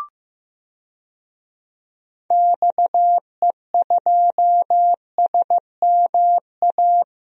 Morse code, heard as a keyed tone, about 15 wpm, 700 Hz. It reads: XE2SMA
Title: DXCC-100 Call Signs - 6 Characters - 15wpm